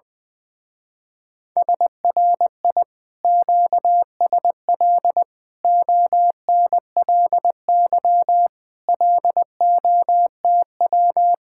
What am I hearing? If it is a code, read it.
Morse code, heard as a keyed tone, about 20 wpm, 700 Hz. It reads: SRI QSL ONLY LOTW
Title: Speed-Racing Common QSO Phrases - 20wpm